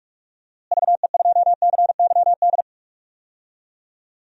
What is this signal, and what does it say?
Morse code, heard as a keyed tone, about 46 wpm, 700 Hz. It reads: VE2CYD
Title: DXCC-100 Call Signs - 6 Characters - Repeated 2x - 45wpm